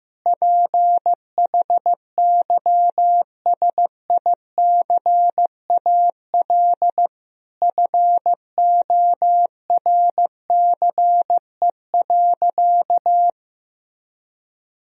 Morse code, heard as a keyed tone, about 16 wpm, 700 Hz. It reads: PHYSICAL FORCE.
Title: Sentences from Top 1000 Words - Review Part 4 - 15wpm